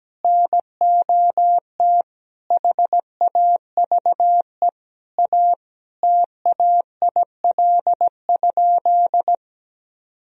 Morse code, heard as a keyed tone, about 17 wpm, 700 Hz. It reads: NOT HAVE A TAIL?
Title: Sentences from Top 1000 Words - Review Part 3 - 17wpm